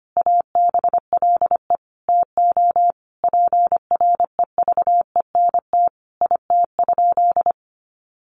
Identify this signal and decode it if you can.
Morse code, heard as a keyed tone, about 25 wpm, 700 Hz. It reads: ABLE TO PREVENT IT?
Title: Sentences from Top 1100 Words 25wpm